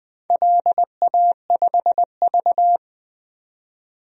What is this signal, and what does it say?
Morse code, heard as a keyed tone, about 20 wpm, 700 Hz. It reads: LA5V
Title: DXCC-100 Call Signs - 4 Characters - 20wpm